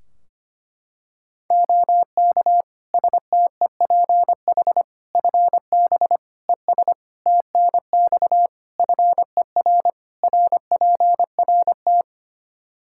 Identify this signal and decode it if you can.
Morse code, heard as a keyed tone, about 25 wpm, 700 Hz. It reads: OK STEPH FB ES TNX FER RPRT
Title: Practice QSO - Exchange 2 - Info - 25wpm